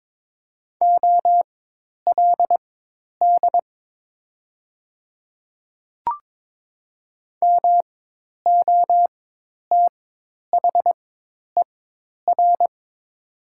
Morse code, heard as a keyed tone, about 22 wpm, spaced as for 10 wpm, 700 Hz. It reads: OLD MOTHER
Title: 4x Character Spacing - Top 200 Words - 22wpm